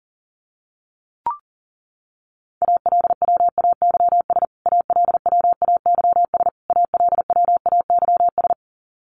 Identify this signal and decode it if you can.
Morse code, heard as a keyed tone, about 40 wpm, 700 Hz. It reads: ALWAYS ALWAYS ALWAYS
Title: Top 300 Words Repeated 40wpm